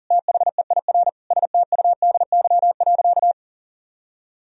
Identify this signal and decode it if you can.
Morse code, heard as a keyed tone, about 40 wpm, 700 Hz. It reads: THEIR STUDY.
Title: Sentences from Top 1000 Words - Review Part 3 - 40wpm